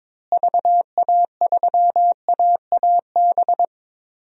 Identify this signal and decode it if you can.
Morse code, heard as a keyed tone, about 22 wpm, 700 Hz. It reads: VA3AAB
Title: DXCC-100 Call Signs - 6 Characters - Repeated 3x - 22wpm